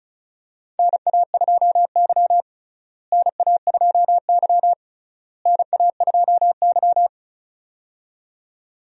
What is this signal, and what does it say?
Morse code, heard as a keyed tone, about 34 wpm, 700 Hz. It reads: NA2Y NA2Y NA2Y
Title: USA Call Signs - 4 Characters - Repeated 3x - 35wpm